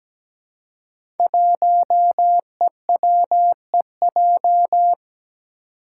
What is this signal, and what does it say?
Morse code, heard as a keyed tone, about 17 wpm, 700 Hz. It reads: 1EWEJ
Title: Five Challenging Letter-Number Combinations 17wpm